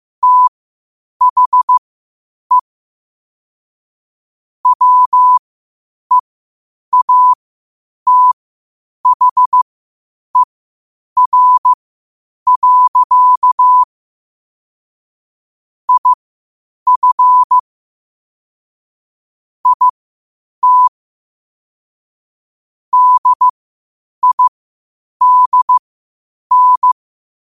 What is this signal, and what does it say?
Morse code, heard as a keyed tone, about 15 wpm, spaced as for 8 wpm, 1000 Hz. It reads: THE WEATHER. IF IT DIDN
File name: qotdfiles / 8wpm